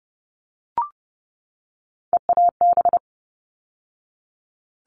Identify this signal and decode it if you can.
Morse code, heard as a keyed tone, about 30 wpm, 700 Hz. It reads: EAB
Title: All Cut Numbers - 3-Digit 30wpm